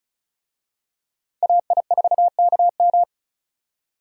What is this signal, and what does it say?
Morse code, heard as a keyed tone, about 36 wpm, 700 Hz. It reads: AI4KM
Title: USA Call Signs - 5 Characters - 35wpm